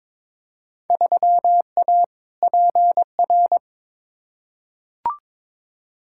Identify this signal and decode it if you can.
Morse code, heard as a keyed tone, about 22 wpm, 700 Hz. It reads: 3A PR
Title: Field Day Exchanges Encore - 22wpm